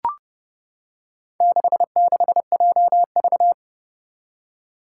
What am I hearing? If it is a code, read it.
Morse code, heard as a keyed tone, about 30 wpm, 700 Hz. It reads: 66JV
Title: Four Challenging Letter-Number Combinations 30wpm